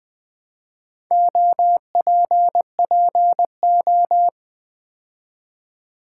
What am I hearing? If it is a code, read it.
Morse code, heard as a keyed tone, about 20 wpm, 700 Hz. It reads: OPPO
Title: Build A Word - 2 Letters At A Time - 8-Letter Words - 20wpm